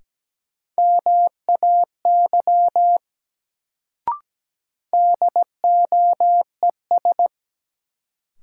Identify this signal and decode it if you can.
Morse code, heard as a keyed tone, about 17 wpm, 700 Hz. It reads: MAY DOES
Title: Sets of 2 Words - New Word Drill - 17wpm